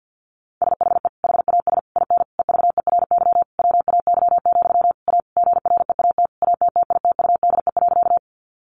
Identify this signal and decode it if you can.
Morse code, heard as a keyed tone, about 50 wpm, 700 Hz. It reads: SHE HAS IN EVERY WAY, A GREAT ATTITUDE.
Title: Sentences from Top 1300 Words 50wpm